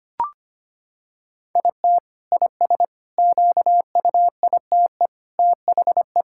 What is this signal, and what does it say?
Morse code, heard as a keyed tone, about 25 wpm, 700 Hz. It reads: IT IS QUITE THE
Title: Sentences from Top 1000 Words - Review Part 5 - 25wpm